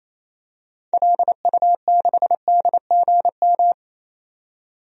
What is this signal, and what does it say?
Morse code, heard as a keyed tone, about 28 wpm, 700 Hz. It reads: LU6DGM
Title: DXCC-100 Call Signs - 6 Characters - 28wpm